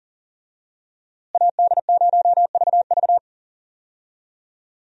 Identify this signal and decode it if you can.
Morse code, heard as a keyed tone, about 40 wpm, 700 Hz. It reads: AD0VV
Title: USA Call Signs - 5 Characters - Repeated 2x - 40wpm